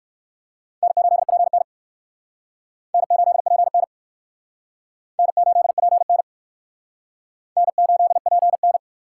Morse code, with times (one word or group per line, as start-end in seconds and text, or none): 0.8-1.6 N8PN
2.9-3.8 N8PN
5.2-6.2 N8PN
7.6-8.8 N8PN